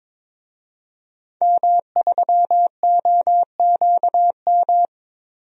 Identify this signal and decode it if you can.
Morse code, heard as a keyed tone, about 22 wpm, 700 Hz. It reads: M3OQM
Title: DXCC-100 Call Signs - 5 Characters - Repeated 3x - 22wpm